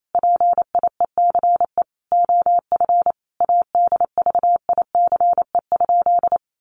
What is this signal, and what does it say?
Morse code, heard as a keyed tone, about 28 wpm, 700 Hz. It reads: PIECE OF ADVICE?
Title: Sentences from Top 1100 Words 28wpm